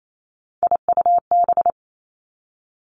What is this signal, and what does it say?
Morse code, heard as a keyed tone, about 28 wpm, 700 Hz. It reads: IUB
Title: Three Challenging Letter-Number Combinations 28wpm